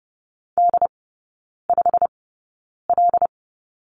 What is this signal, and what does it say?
Morse code, heard as a keyed tone, about 30 wpm, spaced as for 9 wpm, 700 Hz. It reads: D5L
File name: course / 30:10wpm / 087 Lesson 15e - Number 5 - Three Characters 30wpm